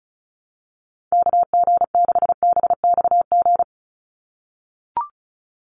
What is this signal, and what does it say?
Morse code, heard as a keyed tone, about 36 wpm, 700 Hz. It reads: KG6BXG
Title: USA Call Signs 35wpm